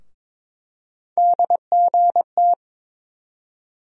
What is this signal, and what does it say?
Morse code, heard as a keyed tone, about 22 wpm, 700 Hz. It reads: DGT
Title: USA Call Sign Postfixes - 3 Letters 22wpm